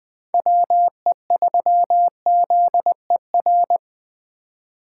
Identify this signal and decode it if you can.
Morse code, heard as a keyed tone, about 24 wpm, 700 Hz. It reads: WE3ZER
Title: Speed Racing - USA 2x3 Call Signs 20wpm